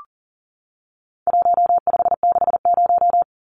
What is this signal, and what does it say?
Morse code, heard as a keyed tone, about 40 wpm, 700 Hz. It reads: 1560
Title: Numbers - Contest Style - 4-Digit 40wpm